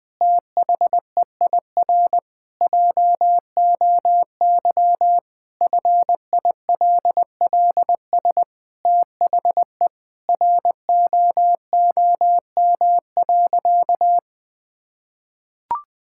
Morse code, heard as a keyed tone, about 20 wpm, 700 Hz. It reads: THEIR JOY FILLS THE ROOM.
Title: Sentences with 2-Word Phrases - Spoken Beginning - 20wpm